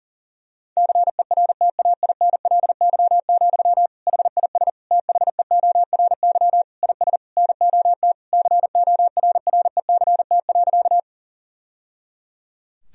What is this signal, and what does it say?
Morse code, heard as a keyed tone, about 40 wpm, 700 Hz. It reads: CERTAINLY, HIS THEORY IS NOT CORRECT.
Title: Sentences from Top 1000 Words - Review Part 3 - 40wpm